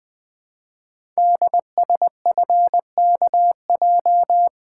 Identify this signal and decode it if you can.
Morse code, heard as a keyed tone, about 20 wpm, 700 Hz. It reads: DSFKJ
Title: Code Groups - 5 Letters - 20wpm